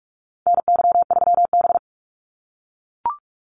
Morse code, heard as a keyed tone, about 44 wpm, 700 Hz. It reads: NY3B
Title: Call Signs - 45wpm